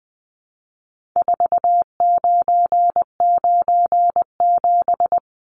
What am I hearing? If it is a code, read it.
Morse code, heard as a keyed tone, about 20 wpm, 700 Hz. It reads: 4997
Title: Numbers - 4-Digit 20wpm